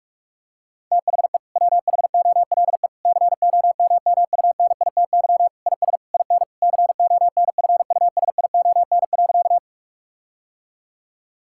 Morse code, heard as a keyed tone, about 41 wpm, 700 Hz. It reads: THE WHOLE COMMUNITY IS IN CONFUSION.
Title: Sentences from Top 1600 Words 45wpm